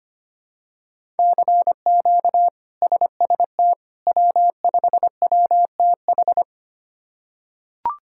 Morse code, heard as a keyed tone, about 25 wpm, 700 Hz. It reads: CQ SST W5WTH
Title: K1USN SST QSO Practice - 25wpm